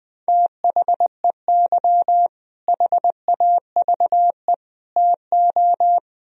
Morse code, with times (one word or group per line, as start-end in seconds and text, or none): 0.3-2.3 THEY
2.7-4.5 HAVE
5.0-6.0 TO